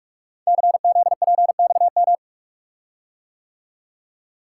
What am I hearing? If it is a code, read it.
Morse code, heard as a keyed tone, about 44 wpm, 690 Hz. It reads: CZPXM
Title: Code Groups - 5 Letters - 45wpm